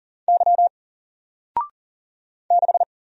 Morse code, heard as a keyed tone, about 40 wpm, 700 Hz. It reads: Y 6
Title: Single Character (Letter:Number:Punctuation:Prosign) 40wpm